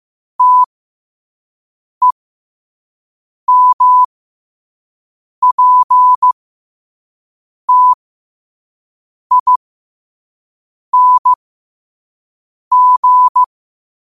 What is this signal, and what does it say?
Morse code, heard as a keyed tone, about 15 wpm, spaced as for 5 wpm, 1000 Hz. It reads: TEMPTING
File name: qotdfiles / 5wpm